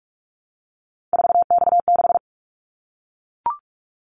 Morse code, heard as a keyed tone, about 44 wpm, 700 Hz. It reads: 4X6
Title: Callsign Prefixes for Top 50 DXCC - 45wpm